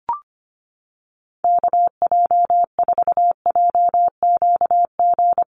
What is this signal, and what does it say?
Morse code, heard as a keyed tone, about 25 wpm, 700 Hz. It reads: KJ4JQG
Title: USA Call Signs Repeated 3x 25wpm